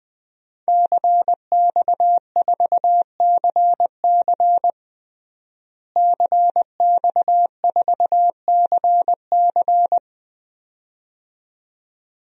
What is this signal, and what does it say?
Morse code, heard as a keyed tone, about 20 wpm, 700 Hz. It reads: CX4CC CX4CC
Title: DXCC-100 Call Signs - 5 Characters - Repeated 2x - 20wpm